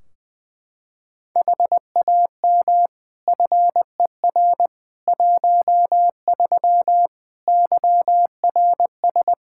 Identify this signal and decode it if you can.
Morse code, heard as a keyed tone, about 20 wpm, 700 Hz. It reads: HAM FER 13 YRS
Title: Practice QSO - Exchange 3 - More Info - 20wpm